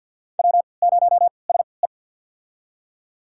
Morse code, heard as a keyed tone, about 48 wpm, spaced as for 28 wpm, 700 Hz. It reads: W0SE
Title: USA Call Signs - 4 Characters - 3x Character Spacing - 50wpm